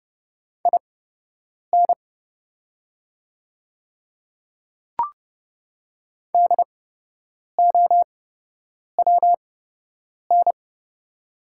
Morse code, heard as a keyed tone, about 30 wpm, spaced as for 8 wpm, 700 Hz. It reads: IN DOWN
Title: 8x Character Spacing - Top 200 Words - 30wpm